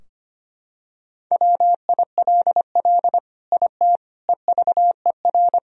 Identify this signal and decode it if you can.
Morse code, heard as a keyed tone, about 25 wpm, 700 Hz. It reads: WILL IT EVER
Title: Sentences from Top 1000 Words - Review Part 2 - 25wpm